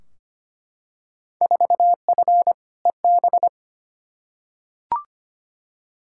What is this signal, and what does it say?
Morse code, heard as a keyed tone, about 25 wpm, 700 Hz. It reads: 4F EB
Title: Field Day Exchanges - 25wpm